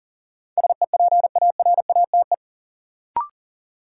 Morse code, heard as a keyed tone, about 40 wpm, 700 Hz. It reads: SEPARATE
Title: Sentences from Top 1000 Words - Review Part 7 - New Word Drill - 40wpm